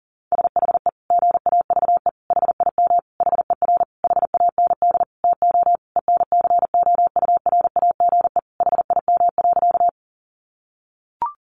Morse code, heard as a keyed tone, about 37 wpm, 700 Hz. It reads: SHE GAVE HIM HER HAND TO ENCOURAGE HIM.
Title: Sentences from Top 1400 Words 40wpm